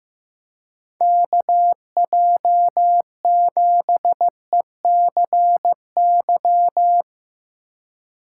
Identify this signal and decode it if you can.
Morse code, heard as a keyed tone, about 15 wpm, 700 Hz. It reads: KJ7ECY
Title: USA Call Signs - 6 Characters - 3x Character Spacing - 15wpm